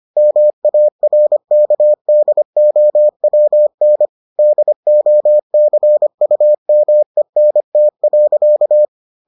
Morse code, heard as a keyed tone, about 25 wpm, 600 Hz. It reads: MARKDOWN DOCUMENT.